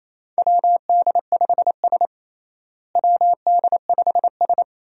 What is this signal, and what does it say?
Morse code, heard as a keyed tone, about 28 wpm, 700 Hz. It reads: WD5S WD5S
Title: USA Call Signs - 4 Characters - Repeated 2x - 28wpm